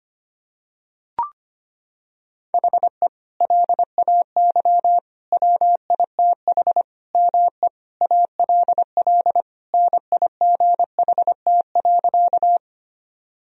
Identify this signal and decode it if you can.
Morse code, heard as a keyed tone, about 25 wpm, 700 Hz. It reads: HE LAY WITH ME ALL NIGHT.
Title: Sentences from Top 1000 Words - Review Part 2 - 25wpm